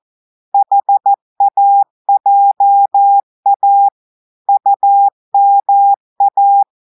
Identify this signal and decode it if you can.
Morse code, heard as a keyed tone, about 14 wpm, 800 Hz. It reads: HAJA UMA
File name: audios / portugais / portugais texte en clair 2 vitesse 14 mots minutes